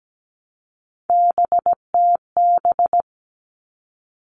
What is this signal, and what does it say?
Morse code, heard as a keyed tone, about 17 wpm, 700 Hz. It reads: BTB
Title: All Cut Numbers - 3-Digit 17wpm